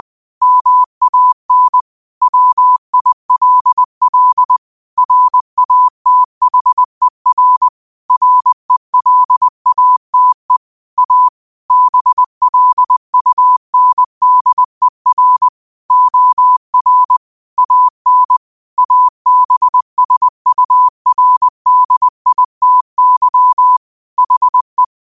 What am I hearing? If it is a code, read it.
Morse code, heard as a keyed tone, about 20 wpm, 1000 Hz. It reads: MAN WILL RATHER RELATE A BLUNDER OR AN ABSURDITY HE
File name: qotdfiles / 20wpm